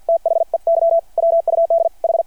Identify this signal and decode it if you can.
Morse code, heard as a keyed tone, about 53 wpm, 690 Hz. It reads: THEY WUD H
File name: archived / CRSnet